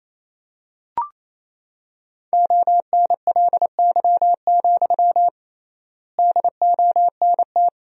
Morse code, heard as a keyed tone, about 28 wpm, 700 Hz. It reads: ONLY, DONT
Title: 3x Word-Spacing - Sentences from Top 300 Words - 28wpm